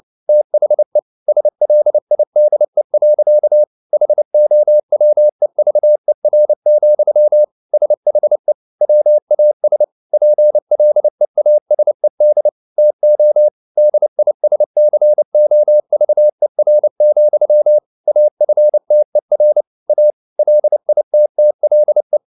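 Morse code, heard as a keyed tone, about 29 wpm, 600 Hz. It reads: THE SLIDE. HOWEVER, SHE WAS PLEASED TO DISCOVER, AFTER A LITTLE
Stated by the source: Morse code at 29 wpm, sending Ken Follett's novel Never in English, from CW med Gnister.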